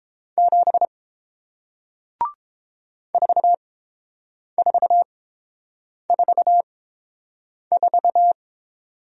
Morse code, heard as a keyed tone, about 31 wpm, spaced as for 7 wpm, 700 Hz. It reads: 7 4444